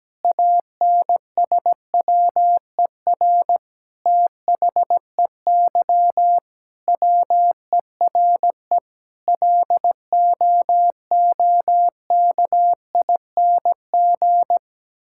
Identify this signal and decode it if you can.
Morse code, heard as a keyed tone, about 17 wpm, 700 Hz. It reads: ANSWER THEY WERE LOOKING
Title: Sentences from Top 1000 Words - Review Part 6 - 17wpm